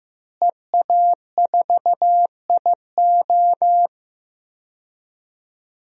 Morse code, heard as a keyed tone, about 15 wpm, 700 Hz. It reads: EA4IO
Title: DXCC-100 Call Signs - 5 Characters - Repeated 2x - 15wpm